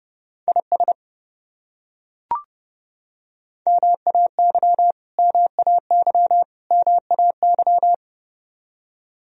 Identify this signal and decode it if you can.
Morse code, heard as a keyed tone, about 32 wpm, 700 Hz. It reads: IS MAY MAY MAY
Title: Top 100 Words Repeated 30wpm